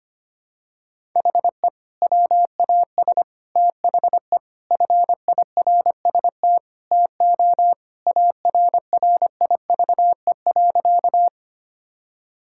Morse code, heard as a keyed tone, about 25 wpm, 700 Hz. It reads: HE WAS THE FIRST TO ARRIVE.
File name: practice / 25wpm / Sentences with 2-Word Phrases - 25wpm